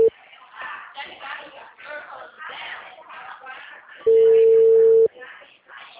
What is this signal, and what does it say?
Ringback tone (China, 450 Hz), with speech.